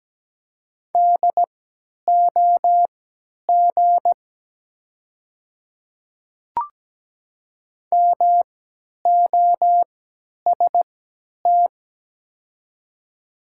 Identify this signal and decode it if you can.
Morse code, heard as a keyed tone, about 17 wpm, spaced as for 10 wpm, 700 Hz. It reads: DOG MOST
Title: 3x Character Spacing - Top 500 Words - 17wpm